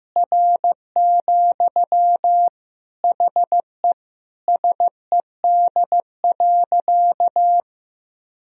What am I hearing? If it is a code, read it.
Morse code, heard as a keyed tone, about 15 wpm, 700 Hz. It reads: R, HE SED.
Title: Sentences from Top 100 Words and Abbreviations 15wpm